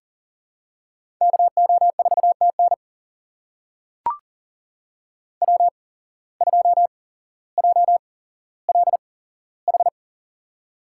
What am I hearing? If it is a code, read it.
Morse code, heard as a keyed tone, about 40 wpm, spaced as for 11 wpm, 700 Hz. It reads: KO4T NW2JLH
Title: USA Call Signs - 8x Character Spacing - 40wpm